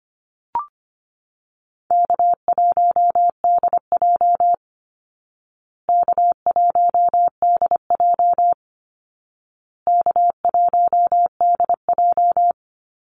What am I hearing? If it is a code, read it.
Morse code, heard as a keyed tone, about 25 wpm, 700 Hz. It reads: K1DJ K1DJ K1DJ
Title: Easy Call Signs Repeated 3x - 25wpm